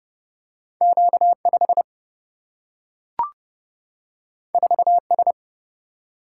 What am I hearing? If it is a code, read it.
Morse code, heard as a keyed tone, about 31 wpm, 700 Hz. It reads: Q 5 4 S
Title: Single-Letter Single-Number 30wpm